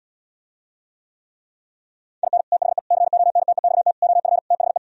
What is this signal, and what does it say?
Morse code, heard as a keyed tone, about 77 wpm, 700 Hz. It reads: IN THE CONTEXT OF THE